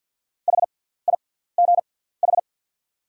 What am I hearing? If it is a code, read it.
Morse code, heard as a keyed tone, about 48 wpm, spaced as for 17 wpm, 690 Hz. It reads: HIGH
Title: 6x Character Spacing - Top 200 Words - 50wpm